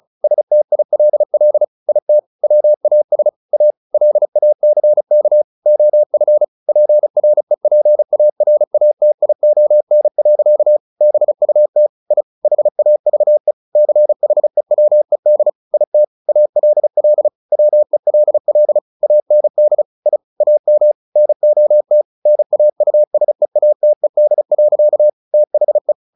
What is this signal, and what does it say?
Morse code, heard as a keyed tone, about 34 wpm, 600 Hz. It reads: STILL IT WAS A LACK OF PREPARATION. BUT I HAVE CHEWED IT ALL WELL AND I AM NOT NAUSEATED. THE
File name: the old man and the sea 2 35wpm